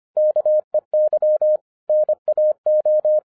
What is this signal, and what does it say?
Morse code, heard as a keyed tone, about 25 wpm, 610 Hz. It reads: KEY NAO
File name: 2011-10-mp3 / n14-MORSE EXPRESS 2011 CHRISTMAS KEY 25wpm